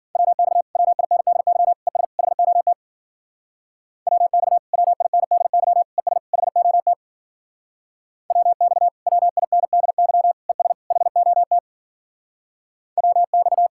Morse code, heard as a keyed tone, about 51 wpm, 700 Hz. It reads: WX WINDY ES HOT WX WINDY ES HOT WX WINDY ES HOT WX